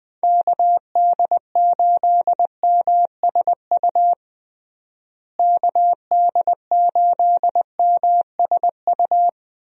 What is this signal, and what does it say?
Morse code, heard as a keyed tone, about 21 wpm, 700 Hz. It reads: KD8MSU KD8MSU
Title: USA Call Signs - 6 Characters - Repeated 3x - 20wpm